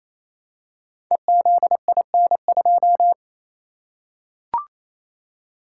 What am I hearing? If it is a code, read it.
Morse code, heard as a keyed tone, about 27 wpm, 700 Hz. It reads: EZIN2
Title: Code Groups - 5 Letter:Numbers - 28wpm